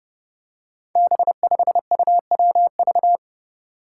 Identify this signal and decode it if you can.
Morse code, heard as a keyed tone, about 28 wpm, 700 Hz. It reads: B5UWV
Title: Five Challenging Letter-Number Combinations 30wpm